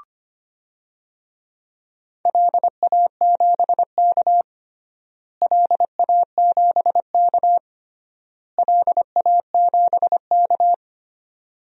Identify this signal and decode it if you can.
Morse code, heard as a keyed tone, about 25 wpm, 700 Hz. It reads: LA7K LA7K LA7K
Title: DXCC-100 Call Signs - 4 Characters - Repeated 3x - 25wpm